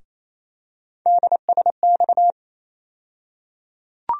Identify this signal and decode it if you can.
Morse code, heard as a keyed tone, about 28 wpm, 700 Hz. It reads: DSX
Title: USA Call Sign Postfixes - 3 Letters 28wpm